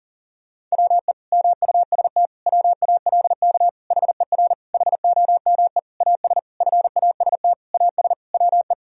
Morse code, heard as a keyed tone, about 37 wpm, 700 Hz. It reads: WE MUST WALK HER HOME AS FAST AS WE
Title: Sentences from Top 1000 Words - Review Part 2 - 40wpm